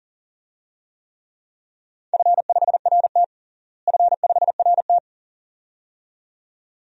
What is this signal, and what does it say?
Morse code, heard as a keyed tone, about 40 wpm, 700 Hz. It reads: F5RT F5RT
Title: DXCC-100 Call Signs - 4 Characters - Repeated 2x - 40wpm